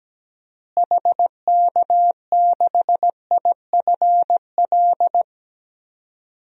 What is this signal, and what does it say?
Morse code, heard as a keyed tone, about 17 wpm, 700 Hz. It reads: HK6IFL
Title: DXCC-100 Call Signs - 6 Characters - Repeated 2x - 17wpm